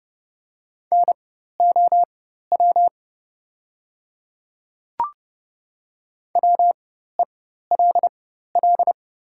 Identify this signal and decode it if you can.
Morse code, heard as a keyed tone, about 30 wpm, spaced as for 14 wpm, 700 Hz. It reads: NOW WELL
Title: 4x Character Spacing - Top 300 Words - 30wpm